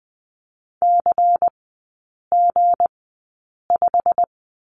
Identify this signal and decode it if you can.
Morse code, heard as a keyed tone, about 20 wpm, spaced as for 8 wpm, 700 Hz. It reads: CG5
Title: 159 Lesson 26e - Letter g - Three Characters 20wpm